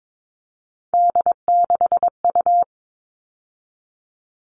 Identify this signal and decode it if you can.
Morse code, heard as a keyed tone, about 22 wpm, 700 Hz. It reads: D6U
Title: All Cut Numbers - 3-Digit 22wpm